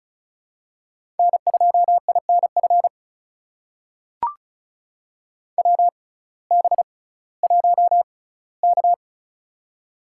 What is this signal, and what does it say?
Morse code, heard as a keyed tone, about 36 wpm, spaced as for 19 wpm, 700 Hz. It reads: N2INF WB1K
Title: USA Call Signs - 6x Character Spacing - 35wpm